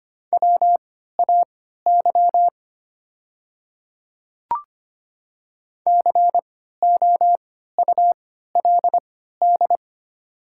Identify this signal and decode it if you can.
Morse code, heard as a keyed tone, about 25 wpm, spaced as for 14 wpm, 700 Hz. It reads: WAY COULD
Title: 3x Character Spacing - Top 100 Words - 25wpm